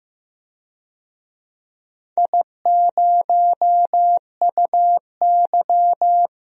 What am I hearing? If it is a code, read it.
Morse code, heard as a keyed tone, about 15 wpm, 700 Hz. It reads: I0UY
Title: DXCC-100 Call Signs - 4 Characters - 15wpm